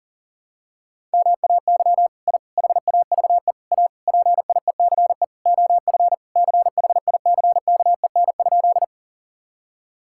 Morse code, heard as a keyed tone, about 40 wpm, 700 Hz. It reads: MAY I HAVE A PIECE OF CHICKEN?
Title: Sentences from Top 1000 Words - Review Part 5 - 40wpm